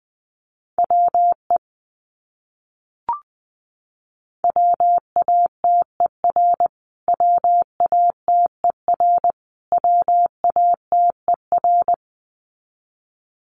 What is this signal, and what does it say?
Morse code, heard as a keyed tone, about 20 wpm, 700 Hz. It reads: WE WATER WATER WATER
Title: Top 500 Words Repeated 20wpm